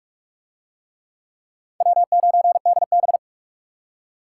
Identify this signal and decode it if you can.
Morse code, heard as a keyed tone, about 41 wpm, 700 Hz. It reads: W9DB